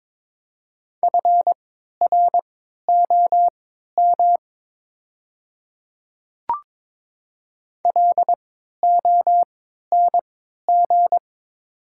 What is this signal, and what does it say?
Morse code, heard as a keyed tone, about 22 wpm, spaced as for 12 wpm, 700 Hz. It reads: FROM LONG
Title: 3x Character Spacing - Top 200 Words - 22wpm